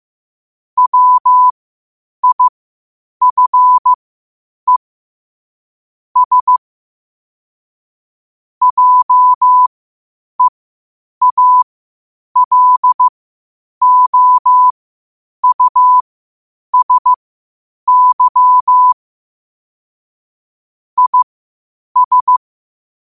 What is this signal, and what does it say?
Morse code, heard as a keyed tone, about 15 wpm, spaced as for 8 wpm, 1000 Hz. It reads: WIFE S JEALOUSY IS
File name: qotdfiles / 8wpm